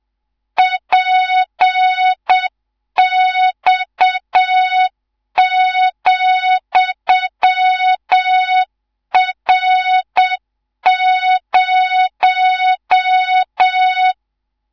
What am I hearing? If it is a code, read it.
Morse code, heard as a keyed tone, about 7 wpm, 3700 Hz, with noise. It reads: PX,R0
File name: Morze-HA2MN / hang / 19 GYAKORLAT